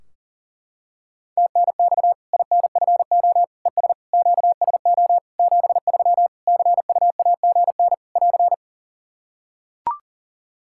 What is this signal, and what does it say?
Morse code, heard as a keyed tone, about 39 wpm, 700 Hz. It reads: TNX INFO ES QSO 73 CUAGN <AR>
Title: Practice QSO - Exchange 4 - Ending - 40wpm